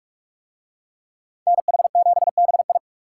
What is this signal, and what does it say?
Morse code, heard as a keyed tone, about 41 wpm, 700 Hz. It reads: NH7BI